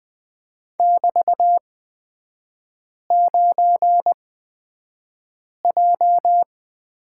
Morse code, heard as a keyed tone, about 20 wpm, spaced as for 5 wpm, 700 Hz. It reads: <BT>9J
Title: 224 Lesson 36e - BT - Three Characters 20wpm